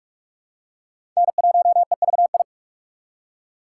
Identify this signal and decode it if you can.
Morse code, heard as a keyed tone, about 47 wpm, 700 Hz. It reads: N1EVI
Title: USA Call Signs - 5 Characters - 2x Character Spacing - 45wpm